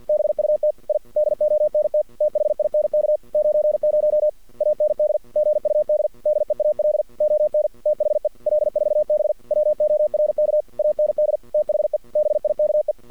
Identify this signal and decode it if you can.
Morse code, heard as a keyed tone, about 53 wpm, 610 Hz. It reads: BUT I DONT THINK 90 AND GUD DEAL ON THE L4B WORK AND THE BIKE
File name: archived / CRSnet